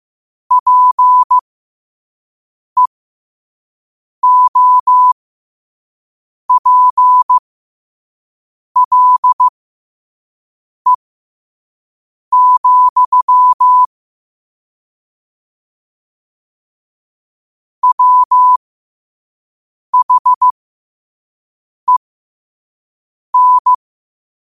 Morse code, heard as a keyed tone, about 15 wpm, spaced as for 5 wpm, 1000 Hz. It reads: PEOPLE, WHEN
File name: qotdfiles / 5wpm